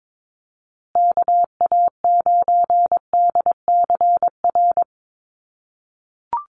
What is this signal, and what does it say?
Morse code, heard as a keyed tone, about 22 wpm, 700 Hz. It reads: KA9DCR
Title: USA Call Signs Repeated 2x 22wpm